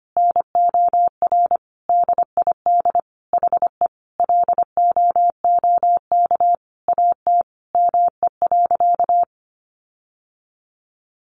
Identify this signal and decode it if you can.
Morse code, heard as a keyed tone, about 25 wpm, 700 Hz. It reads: NOR DID HE LOOK AT ME.
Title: Sentences from Top 1500 Words 25wpm